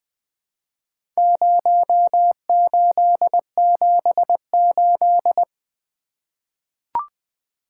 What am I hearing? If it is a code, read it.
Morse code, heard as a keyed tone, about 20 wpm, 700 Hz. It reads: 0878
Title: Four Letter-Number Combinations - 20wpm